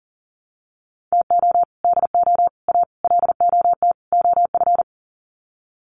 Morse code, heard as a keyed tone, about 38 wpm, 700 Hz. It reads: TO DO A LOT OF
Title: Sets of 5 Words Encore 40wpm